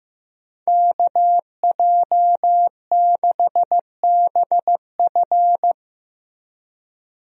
Morse code, heard as a keyed tone, about 15 wpm, 700 Hz. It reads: KJ6BF
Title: USA Call Signs - 5 Characters - Repeated 2x - 15wpm